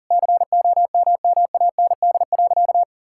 Morse code, heard as a keyed tone, about 40 wpm, 700 Hz. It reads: COMMAND.
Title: Sentences from Top 1600 Words 40wpm